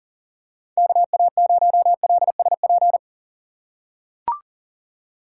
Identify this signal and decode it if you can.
Morse code, heard as a keyed tone, about 40 wpm, 700 Hz. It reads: KA0LSP
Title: USA Call Signs - 6 Characters - 3x Character Spacing - 40wpm